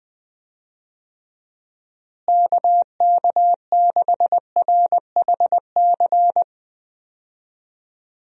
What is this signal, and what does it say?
Morse code, heard as a keyed tone, about 20 wpm, 700 Hz. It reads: KK6RHC
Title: USA Call Signs - 6 Characters - Repeated 3x - 20wpm